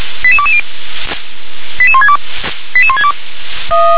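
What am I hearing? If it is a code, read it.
Five-tone selective call (ZVEI), heard as quick runs of five tones, tones of about 70 ms. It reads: 88220 88162 88162
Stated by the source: ZVEI five-tone call 88101, received on SDR.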